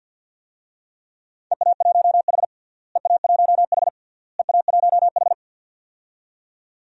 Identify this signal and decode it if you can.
Morse code, heard as a keyed tone, about 46 wpm, 700 Hz. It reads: EA1H EA1H EA1H
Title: DXCC-100 Call Signs - 4 Characters - Repeated 3x - 50wpm